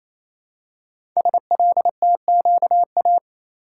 Morse code, heard as a keyed tone, about 27 wpm, 700 Hz. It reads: SLTQA
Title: Code Groups - 5 Letters - 28wpm